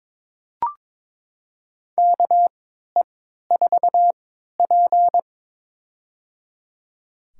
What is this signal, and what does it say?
Morse code, heard as a keyed tone, about 22 wpm, spaced as for 13 wpm, 700 Hz. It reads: KE4P
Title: USA Call Signs - 3x Character Spacing - 22wpm